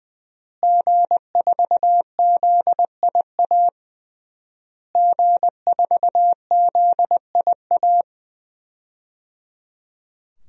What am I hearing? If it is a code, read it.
Morse code, heard as a keyed tone, about 20 wpm, 700 Hz. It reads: G4ZIA G4ZIA
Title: DXCC-100 Call Signs - 5 Characters - Repeated 3x - 20wpm